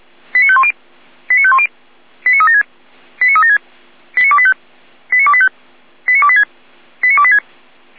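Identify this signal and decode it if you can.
Five-tone selective call (ZVEI), heard as quick runs of five tones, tones of about 70 ms. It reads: 89510 89510 89376 89376 89285 89285 89286 89286
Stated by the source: ZVEI five-tone selective call 89270 received on an SDR.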